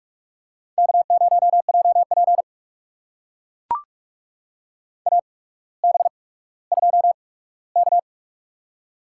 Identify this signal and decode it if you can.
Morse code, heard as a keyed tone, about 44 wpm, spaced as for 19 wpm, 700 Hz. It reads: K0JP AB2K
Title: USA Call Signs - 4 Characters - 8x Character Spacing - 45wpm